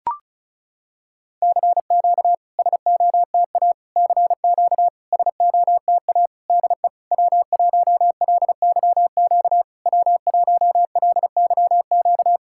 Morse code, heard as a keyed tone, about 36 wpm, 700 Hz. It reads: CQ SOTA CQ SOTA DE W1LYQ W1LYQ
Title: SOTA QSO Practice 35wpm